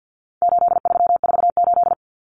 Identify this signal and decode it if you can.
Morse code, heard as a keyed tone, about 50 wpm, 700 Hz. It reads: 8348
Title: Common Cut Numbers - 4-Digit 50wpm